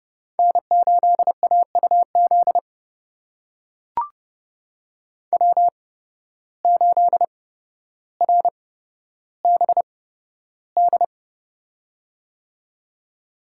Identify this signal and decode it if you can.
Morse code, heard as a keyed tone, about 30 wpm, spaced as for 13 wpm, 700 Hz. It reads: N8AUZ W8RBD
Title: USA Call Signs - 8x Character Spacing - 30wpm